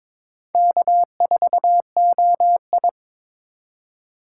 Morse code, heard as a keyed tone, about 26 wpm, 700 Hz. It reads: K4OI
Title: Speed Racing - USA 1x2 Call Signs 22wpm